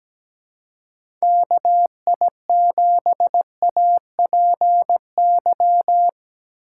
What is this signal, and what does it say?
Morse code, heard as a keyed tone, about 17 wpm, 700 Hz. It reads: KI7APY
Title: USA Call Signs - 4x Character Spacing - 17wpm